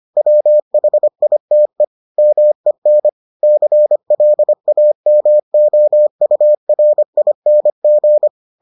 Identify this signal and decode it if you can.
Morse code, heard as a keyed tone, about 25 wpm, 600 Hz. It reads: WHITE MEN CLAMOURING